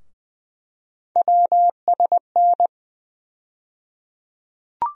Morse code, heard as a keyed tone, about 20 wpm, 700 Hz. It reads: WSN
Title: USA Call Sign Postfixes - 3 Letters 20wpm